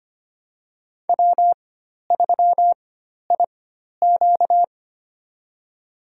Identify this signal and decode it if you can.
Morse code, heard as a keyed tone, about 25 wpm, spaced as for 12 wpm, 700 Hz. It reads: W3IQ
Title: USA Call Signs - 4 Characters - 4x Character Spacing - 25wpm